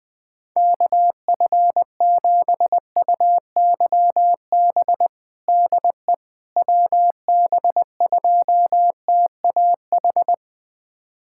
Morse code, heard as a keyed tone, about 20 wpm, 700 Hz. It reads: KF7UYB DE WB2TAH
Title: USA Callsign DE Callsign - 6 Characters - 20wpm